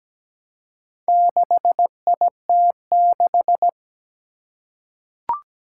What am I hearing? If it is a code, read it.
Morse code, heard as a keyed tone, about 17 wpm, 700 Hz. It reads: 6IT6
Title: Four Challenging Letter-Number Combinations 17wpm